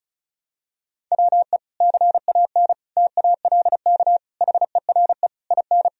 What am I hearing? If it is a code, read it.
Morse code, heard as a keyed tone, about 36 wpm, 700 Hz. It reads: WE CAN TALK HERE IN
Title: Sentences from Top 1000 Words - Review Part 4 - 35wpm